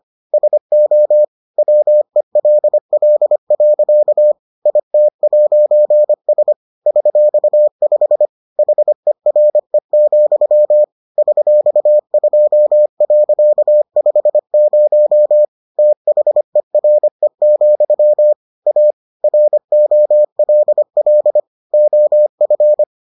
Morse code, heard as a keyed tone, about 25 wpm, 600 Hz. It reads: SO WELL. IT'S $5 HERE, $2.50 THERE, A ROLL OF